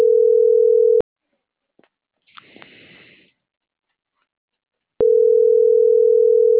Ringback tone (North America), 440 + 480 Hz.